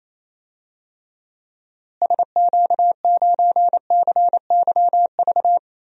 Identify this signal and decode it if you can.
Morse code, heard as a keyed tone, about 28 wpm, 700 Hz. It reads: SQ9CYV
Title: DXCC-100 Call Signs - 6 Characters - Repeated 3x - 28wpm